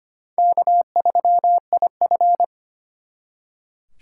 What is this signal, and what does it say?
Morse code, heard as a keyed tone, about 31 wpm, 700 Hz. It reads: K3IF
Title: Speed Racing - USA 1x2 Call Signs 25wpm